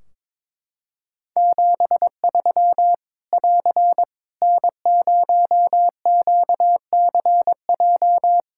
Morse code, heard as a keyed tone, about 22 wpm, 700 Hz. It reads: 73 <AR> N0QCJ
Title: Practice QSO - Exchange 4 - Ending - 22wpm